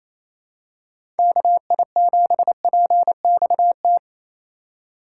Morse code, heard as a keyed tone, about 28 wpm, 700 Hz. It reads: KI7PXT
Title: USA Call Signs - 2x Character Spacing - 28wpm